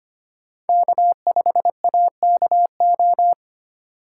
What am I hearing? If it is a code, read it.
Morse code, heard as a keyed tone, about 25 wpm, 700 Hz. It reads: K5AKO
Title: USA Call Signs - 5 Characters - 2x Character Spacing - 25wpm